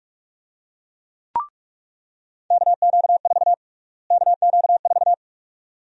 Morse code, heard as a keyed tone, about 46 wpm, 700 Hz. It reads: KQ4 KQ4
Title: USA Call Sign Prefixes - Repeated 2x - 45wpm